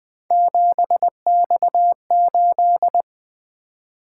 Morse code, heard as a keyed tone, about 20 wpm, 700 Hz. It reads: 7X8
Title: Three Letter-Number Combinations - 20wpm